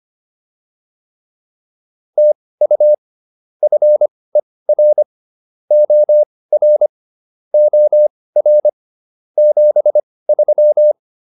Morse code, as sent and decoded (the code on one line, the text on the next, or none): - ..- / ..-. .-. / --- .-. / --- .-. / --... ...--
TU FER OR OR 73